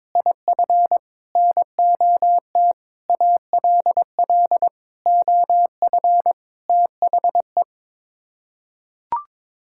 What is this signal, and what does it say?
Morse code, heard as a keyed tone, about 22 wpm, 700 Hz. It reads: IF NOT ALL OF THE
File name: practice / 22wpm / Sets of 6 Words - 22wpm